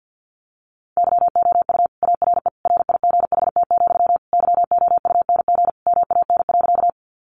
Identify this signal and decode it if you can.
Morse code, heard as a keyed tone, about 50 wpm, 700 Hz. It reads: YOU ARE RIGHT, YOUNG MAN.
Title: Sentences from Top 400 Words 50wpm